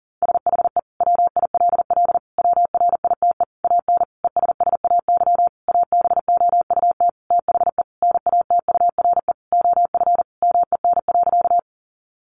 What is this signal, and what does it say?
Morse code, heard as a keyed tone, about 40 wpm, 700 Hz. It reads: SHE WILL WRITE AN ESSAY ABOUT THE NATURE OF MEN.
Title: Sentences from Top 1500 Words 40wpm